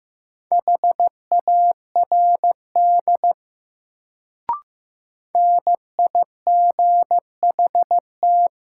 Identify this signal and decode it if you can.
Morse code, heard as a keyed tone, about 15 wpm, 700 Hz. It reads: HARD NIGHT
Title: Sentences from Top 300 Words - New Word Drill - 15wpm